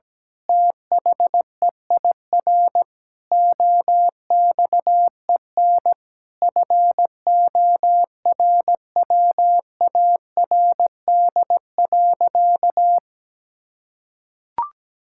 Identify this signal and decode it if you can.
Morse code, heard as a keyed tone, about 17 wpm, 700 Hz. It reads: THEIR OXEN FORWARD.
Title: Pangram Challenge v2 - 17wpm